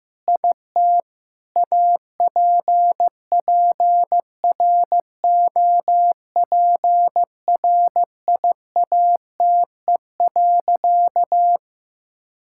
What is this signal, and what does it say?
Morse code, heard as a keyed tone, about 16 wpm, 690 Hz. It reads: IT APPROPRIATE.
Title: Sentences from Top 1000 Words - Review Part 5 - 15wpm